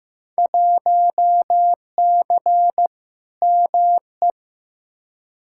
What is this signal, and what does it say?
Morse code, heard as a keyed tone, about 15 wpm, 700 Hz. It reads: 1C ME
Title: Field Day Exchanges - 15wpm